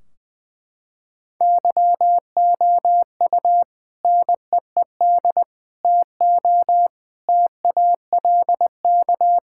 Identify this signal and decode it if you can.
Morse code, heard as a keyed tone, about 20 wpm, 700 Hz. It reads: YOU NEED TO TALK
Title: Sentences from Top 1600 Words 20wpm